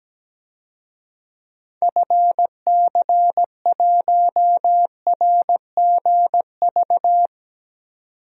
Morse code, heard as a keyed tone, about 17 wpm, 700 Hz. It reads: FC1RGV
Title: DXCC-100 Call Signs - 6 Characters - 17wpm